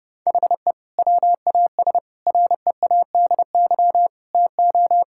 Morse code, heard as a keyed tone, about 32 wpm, 700 Hz. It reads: HE WAS READY TO
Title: Sentences from Top 1000 Words - Review Part 5 - 30wpm